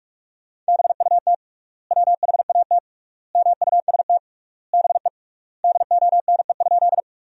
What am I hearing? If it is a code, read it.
Morse code, heard as a keyed tone, about 46 wpm, 700 Hz. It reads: BUT WHAT MUST BE DONE?
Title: 3x Word-Spacing - Sentences from Top 500 Words - 45wpm